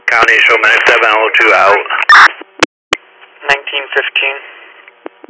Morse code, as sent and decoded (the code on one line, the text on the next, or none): none